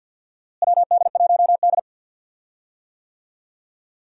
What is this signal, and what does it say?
Morse code, heard as a keyed tone, about 52 wpm, 700 Hz. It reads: WD1D
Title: USA Call Signs - 4 Characters - Repeated 3x - 50wpm